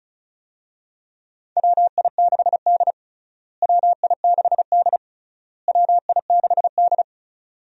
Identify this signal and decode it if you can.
Morse code, heard as a keyed tone, about 36 wpm, 700 Hz. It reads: WI6D WI6D WI6D
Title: USA Call Signs - 4 Characters - Repeated 3x - 35wpm